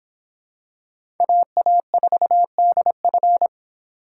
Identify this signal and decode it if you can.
Morse code, heard as a keyed tone, about 25 wpm, 700 Hz. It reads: AA4DF